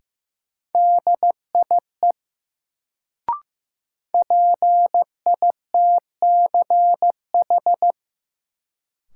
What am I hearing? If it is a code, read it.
Morse code, heard as a keyed tone, about 15 wpm, 700 Hz. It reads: DIE PITCH
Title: Sentences from Top 1300 Words - New Word Drill - 15wpm